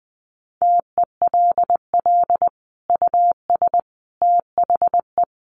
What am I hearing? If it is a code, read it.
Morse code, heard as a keyed tone, about 20 wpm, 700 Hz. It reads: TELL US THE
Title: Sentences from Top 1400 Words 20wpm